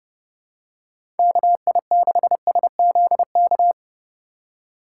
Morse code, heard as a keyed tone, about 32 wpm, 700 Hz. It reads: KI6SZK
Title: USA Call Signs - 6 Characters - 4x Character Spacing - 30wpm